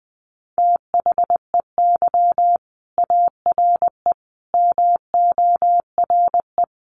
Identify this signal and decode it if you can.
Morse code, heard as a keyed tone, about 20 wpm, 700 Hz. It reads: THEY ARE MORE
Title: Sets of 5 Words Encore2 20wpm